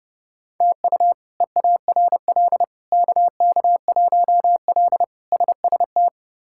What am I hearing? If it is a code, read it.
Morse code, heard as a keyed tone, about 30 wpm, 700 Hz. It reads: TU EARL KK1L SST
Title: K1USN SST QSO Practice - 30wpm